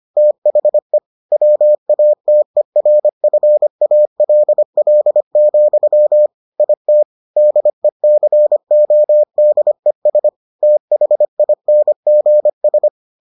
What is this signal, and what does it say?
Morse code, heard as a keyed tone, about 25 wpm, 600 Hz. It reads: THE WATERFALL, IT DECODES THINGS